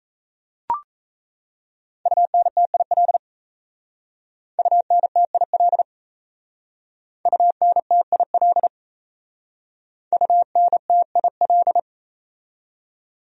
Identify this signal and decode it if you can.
Morse code, heard as a keyed tone, about 34 wpm, 700 Hz. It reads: UNTIL UNTIL UNTIL UNTIL
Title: Speed-Racing Top 500 Words 28wpm